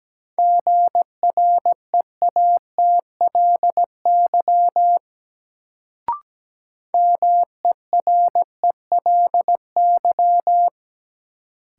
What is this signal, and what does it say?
Morse code, heard as a keyed tone, about 17 wpm, 700 Hz. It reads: GREATLY MERELY
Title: Sentences from Top 1400 Words - New Word Drill - 17wpm